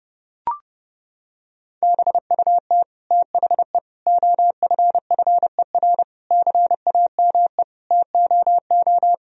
Morse code, heard as a keyed tone, about 30 wpm, 700 Hz. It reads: BUT THE OFFER CAME TOO
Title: Sentences from Top 1000 Words - Review Part 4 - 30wpm